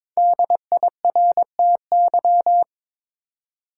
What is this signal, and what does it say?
Morse code, heard as a keyed tone, about 22 wpm, 700 Hz. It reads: DIRTY
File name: 5 Letter Words - First 3 Letters Given - 22wpm